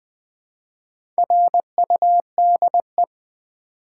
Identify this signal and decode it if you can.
Morse code, heard as a keyed tone, about 20 wpm, 700 Hz. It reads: RUDE
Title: Build A Word - 1 Letter At A Time - 4-Letter Words - 20wpm